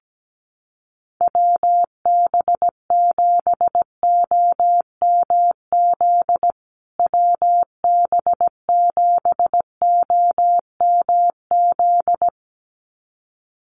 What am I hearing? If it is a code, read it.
Morse code, heard as a keyed tone, about 17 wpm, 700 Hz. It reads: WB7OMZ WB7OMZ
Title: USA Call Signs Repeated 2x 17wpm